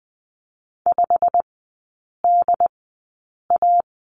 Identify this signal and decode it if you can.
Morse code, heard as a keyed tone, about 20 wpm, spaced as for 8 wpm, 700 Hz. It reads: 5DA
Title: 087 Lesson 15e - Number 5 - Three Characters 20wpm